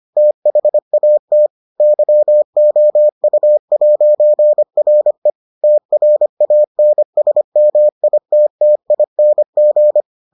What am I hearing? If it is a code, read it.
Morse code, heard as a keyed tone, about 25 wpm, 600 Hz. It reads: THAT YOU'RE TRANSMITTING